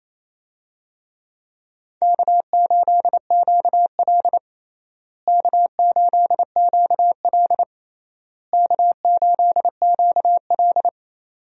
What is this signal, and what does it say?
Morse code, heard as a keyed tone, about 28 wpm, 700 Hz. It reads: K8QL K8QL K8QL
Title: USA Call Signs - 4 Characters - Repeated 3x - 28wpm